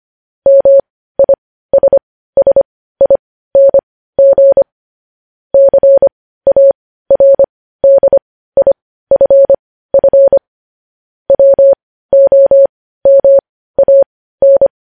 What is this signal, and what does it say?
Morse code, heard as a keyed tone, about 25 wpm, spaced as for 15 wpm, 550 Hz. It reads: MISSING CARDIFF WOMAN
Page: news headlines in morse code at 15 wpm